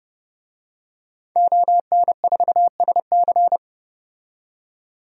Morse code, heard as a keyed tone, about 30 wpm, 700 Hz. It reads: ON4SC
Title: DXCC-100 Call Signs - 5 Characters - 30wpm